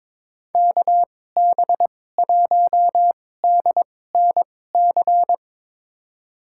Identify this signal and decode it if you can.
Morse code, heard as a keyed tone, about 22 wpm, spaced as for 16 wpm, 700 Hz. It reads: KB1DNC
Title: USA Call Signs - 2x Character Spacing - 22wpm